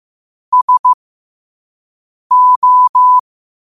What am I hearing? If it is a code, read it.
Morse code, heard as a keyed tone, about 15 wpm, spaced as for 5 wpm, 1000 Hz. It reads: SO
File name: qotdfiles / 5wpm